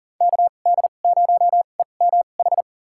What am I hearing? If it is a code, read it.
Morse code, heard as a keyed tone, about 37 wpm, spaced as for 28 wpm, 700 Hz. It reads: KD0EMH
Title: USA Call Signs - 6 Characters - 2x Character Spacing - 40wpm